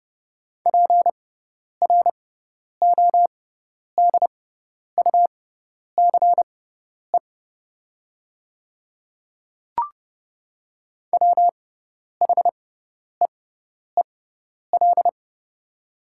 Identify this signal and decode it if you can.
Morse code, heard as a keyed tone, about 30 wpm, spaced as for 10 wpm, 690 Hz. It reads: PRODUCE WHEEL
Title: 6x Character Spacing - Top 500 Words - 30wpm